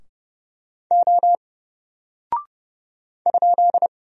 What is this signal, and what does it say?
Morse code, heard as a keyed tone, about 30 wpm, 700 Hz. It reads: O ?
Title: Single Character (Letter:Number:Punctuation:Prosign) 30wpm